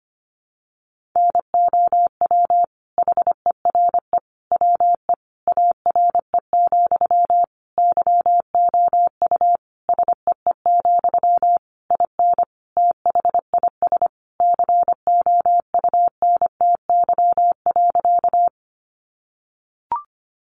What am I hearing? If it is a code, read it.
Morse code, heard as a keyed tone, about 25 wpm, 700 Hz. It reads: NOW HERE WE ARE, YOU SEE, IN THIS COUNTY.
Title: Sentences from Top 1200 Words 25wpm